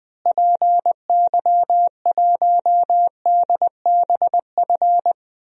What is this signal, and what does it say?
Morse code, heard as a keyed tone, about 20 wpm, 700 Hz. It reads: PY1DBF
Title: DXCC-100 Call Signs - 6 Characters - Repeated 2x - 20wpm